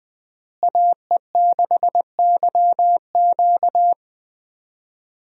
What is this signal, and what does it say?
Morse code, heard as a keyed tone, about 19 wpm, 700 Hz. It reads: AE6YQ
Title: USA Call Signs - 5 Characters - Repeated 3x - 20wpm